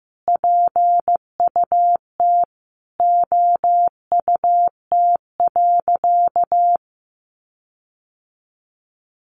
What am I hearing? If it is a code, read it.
Morse code, heard as a keyed tone, about 15 wpm, 700 Hz. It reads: PUT OUT.
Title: Sentences from Top 1400 Words 15wpm